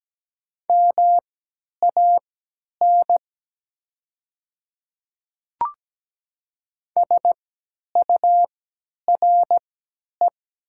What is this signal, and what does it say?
Morse code, heard as a keyed tone, about 17 wpm, spaced as for 10 wpm, 700 Hz. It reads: MAN SURE
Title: 3x Character Spacing - Top 300 Words - 17wpm